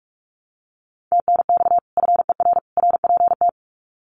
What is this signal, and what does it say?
Morse code, heard as a keyed tone, about 44 wpm, 700 Hz. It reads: TNX FER RPT
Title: Common QSO Phrases 45wpm